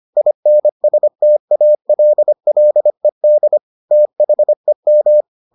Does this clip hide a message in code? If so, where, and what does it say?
Yes: Morse code from 0.2 to 5.2 s, heard as a keyed tone, about 25 wpm, 600 Hz. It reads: INSTALLED THEM